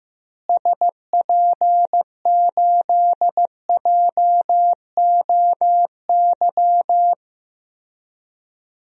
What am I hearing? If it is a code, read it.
Morse code, heard as a keyed tone, about 15 wpm, 700 Hz. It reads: SP8JOY
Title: DXCC-100 Call Signs - 6 Characters - Repeated 2x - 15wpm